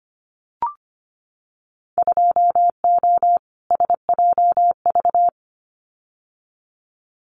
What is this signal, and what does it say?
Morse code, heard as a keyed tone, about 25 wpm, 700 Hz. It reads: 2O SJV
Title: Winter Field Day Exchanges Encore 25wpm